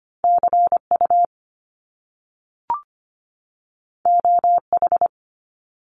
Morse code, heard as a keyed tone, about 25 wpm, 700 Hz. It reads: CU OH
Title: Top Two Letter Combinations in Words 25wpm